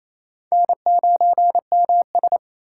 Morse code, heard as a keyed tone, about 28 wpm, 700 Hz. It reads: N9MS
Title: K1USN SST QSO Practice - 28wpm